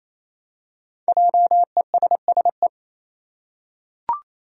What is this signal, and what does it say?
Morse code, heard as a keyed tone, about 27 wpm, 700 Hz. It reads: JESSE
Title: USA Common Names - Max 5 Letters - 28wpm